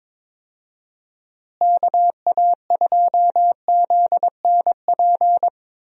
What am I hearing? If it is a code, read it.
Morse code, heard as a keyed tone, about 22 wpm, 700 Hz. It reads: KA2ZNP
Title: USA Call Signs - 6 Characters - Repeated 2x - 22wpm